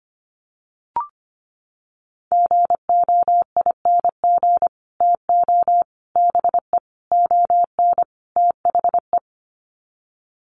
Morse code, heard as a keyed tone, about 25 wpm, 700 Hz. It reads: GOING TO BE ON THE
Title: Sets of 5 Words Encore2 25wpm